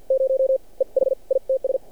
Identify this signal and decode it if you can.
Morse code, heard as a keyed tone, about 47 wpm, spaced as for 40 wpm, 530 Hz. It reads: 0 EH ITS
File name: archived / CRSnet